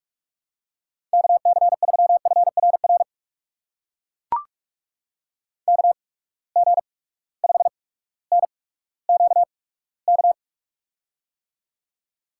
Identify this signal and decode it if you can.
Morse code, heard as a keyed tone, about 44 wpm, spaced as for 19 wpm, 700 Hz. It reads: KC3FRR KG5NQK
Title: USA Call Signs - 6 Characters - 8x Character Spacing - 45wpm